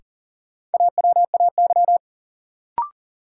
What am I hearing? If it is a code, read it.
Morse code, heard as a keyed tone, about 40 wpm, 700 Hz. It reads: AWAY
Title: Sets of 2 Words - New Word Drill - 40wpm